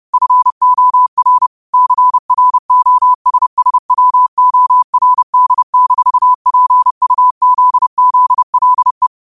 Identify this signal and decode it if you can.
Morse code, heard as a keyed tone, about 30 wpm, 1000 Hz. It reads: FOR CROSSWORD-PUZZLE
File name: qotdfiles / 30wpm